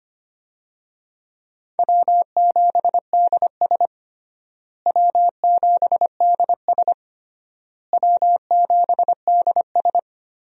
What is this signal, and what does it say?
Morse code, heard as a keyed tone, about 25 wpm, 700 Hz. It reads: W7DS W7DS W7DS
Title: USA Call Signs - 4 Characters - Repeated 3x - 25wpm